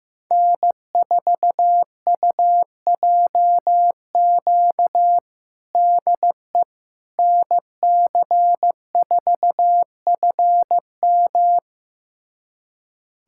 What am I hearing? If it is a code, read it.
Morse code, heard as a keyed tone, about 15 wpm, 700 Hz. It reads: N4UJQ DE NC4FM
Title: USA Callsign DE Callsign - 5 Characters - 15wpm